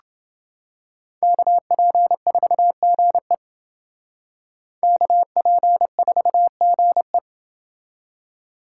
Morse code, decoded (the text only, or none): KP4GE KP4GE